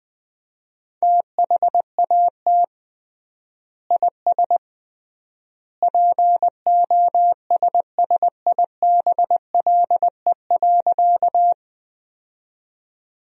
Morse code, heard as a keyed tone, about 20 wpm, 700 Hz. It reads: THAT IS POSSIBLE.
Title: 3x Word-Spacing - Sentences from Top 500 Words - 20wpm